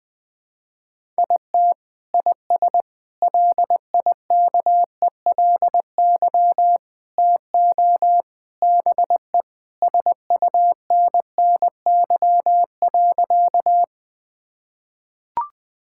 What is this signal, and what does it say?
Morse code, heard as a keyed tone, about 21 wpm, 700 Hz. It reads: IT IS LIKELY TO BE SUNNY.
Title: Sentences with 3-Word Phrases - 20wpm